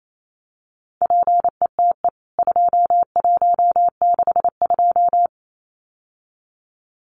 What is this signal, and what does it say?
Morse code, heard as a keyed tone, about 28 wpm, 700 Hz. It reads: PETE 2162
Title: CWT Exchange 28wpm